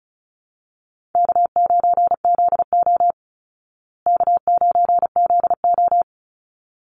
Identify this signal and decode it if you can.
Morse code, heard as a keyed tone, about 34 wpm, 700 Hz. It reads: K9ZO K9ZO
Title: Easy Call Signs Repeated 2x - 35wpm